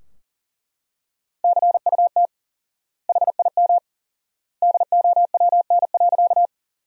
Morse code, heard as a keyed tone, about 40 wpm, 700 Hz. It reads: CUT HIM DOWN.
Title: 4x Word-Spacing - Sentences from Top 300 Words - 40wpm